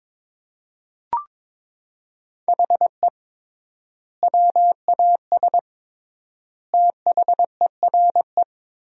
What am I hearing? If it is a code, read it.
Morse code, heard as a keyed tone, about 22 wpm, 700 Hz. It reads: HE WAS THERE
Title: 3x Word-Spacing - Sentences from Top 200 Words - 22wpm